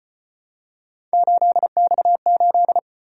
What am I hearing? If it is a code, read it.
Morse code, heard as a keyed tone, about 32 wpm, 700 Hz. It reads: 8X8